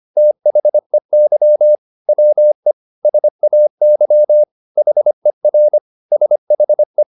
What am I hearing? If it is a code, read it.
Morse code, heard as a keyed tone, about 25 wpm, 600 Hz. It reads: THEY WE SAY HER SHE
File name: Top-100-Words-25